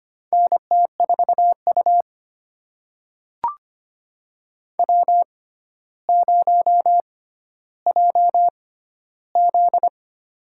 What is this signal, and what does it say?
Morse code, heard as a keyed tone, about 25 wpm, spaced as for 13 wpm, 700 Hz. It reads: NT4U W0JZ
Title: USA Call Signs - 6x Character Spacing - 25wpm